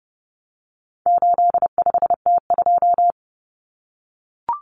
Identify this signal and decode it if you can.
Morse code, heard as a keyed tone, about 29 wpm, 700 Hz. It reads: 85T2
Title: Common Cut Numbers - 4-Digit 30wpm